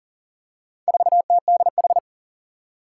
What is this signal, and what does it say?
Morse code, heard as a keyed tone, about 40 wpm, 700 Hz. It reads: 4TDH
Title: Four Challenging Letter-Number Combinations 40wpm